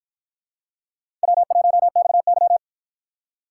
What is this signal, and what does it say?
Morse code, heard as a keyed tone, about 53 wpm, 700 Hz. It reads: W1XY